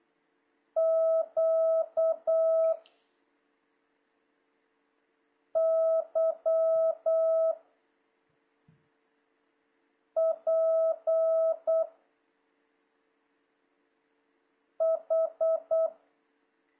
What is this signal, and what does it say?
Morse code, heard as a keyed tone, about 8 wpm, 650 Hz. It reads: Q Y P H
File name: jh8cbh / 2026cw